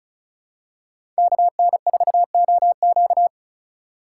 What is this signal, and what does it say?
Morse code, heard as a keyed tone, about 34 wpm, 700 Hz. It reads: KN4OQ
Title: USA Call Signs - 5 Characters - 3x Character Spacing - 35wpm